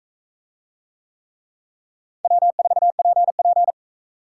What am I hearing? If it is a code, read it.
Morse code, heard as a keyed tone, about 44 wpm, 700 Hz. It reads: W4PP